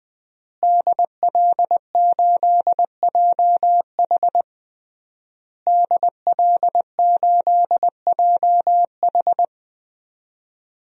Morse code, heard as a keyed tone, about 21 wpm, 700 Hz. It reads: DL8JH DL8JH
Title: DXCC-100 Call Signs - 5 Characters - Repeated 3x - 20wpm